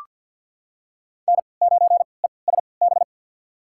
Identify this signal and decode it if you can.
Morse code, heard as a keyed tone, about 53 wpm, spaced as for 29 wpm, 700 Hz. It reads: N9ESB
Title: USA Call Signs - 5 Characters - 3x Character Spacing - 50wpm